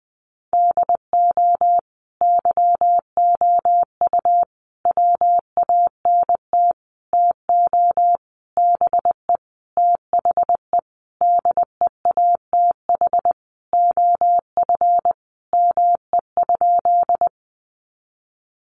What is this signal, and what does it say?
Morse code, heard as a keyed tone, about 20 wpm, 700 Hz. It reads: DO YOU WANT TO BE THE DEATH OF ME?
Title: Sentences from Top 1100 Words 20wpm